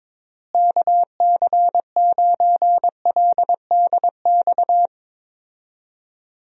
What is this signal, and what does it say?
Morse code, heard as a keyed tone, about 22 wpm, 700 Hz. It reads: KC9LDX
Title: USA Call Signs - 6 Characters - Repeated 3x - 22wpm